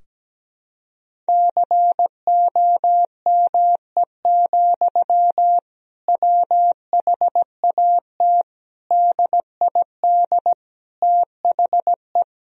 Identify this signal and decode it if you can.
Morse code, heard as a keyed tone, about 17 wpm, 700 Hz. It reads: COME, WHAT DID THE
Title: Sentences from Top 1000 Words - Review Part 3 - 17wpm